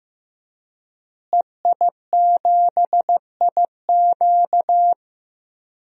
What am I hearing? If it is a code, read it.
Morse code, heard as a keyed tone, about 16 wpm, 700 Hz. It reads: EI7IQ
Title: DXCC-100 Call Signs - 5 Characters - Repeated 3x - 15wpm